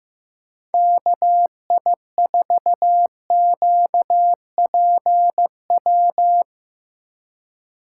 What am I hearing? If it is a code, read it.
Morse code, heard as a keyed tone, about 17 wpm, 700 Hz. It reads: KI4QPW
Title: Speed Racing - USA Call Signs 15wpm